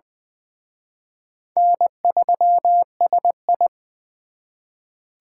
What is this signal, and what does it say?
Morse code, heard as a keyed tone, about 24 wpm, 700 Hz. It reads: N3SI
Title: Speed Racing - USA Call Signs 20wpm